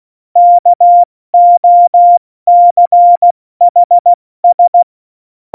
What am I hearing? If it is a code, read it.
Morse code, heard as a keyed tone, about 16 wpm, 700 Hz. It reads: KOCHS